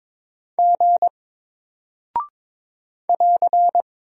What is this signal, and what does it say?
Morse code, heard as a keyed tone, about 22 wpm, 700 Hz. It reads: G <AR>
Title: Single Character (Letter:Number:Punctuation:Prosign) Alt 22wpm